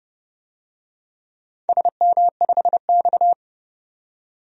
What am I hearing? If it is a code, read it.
Morse code, heard as a keyed tone, about 29 wpm, 700 Hz. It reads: SM5X
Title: DXCC-100 Call Signs - 4 Characters - 30wpm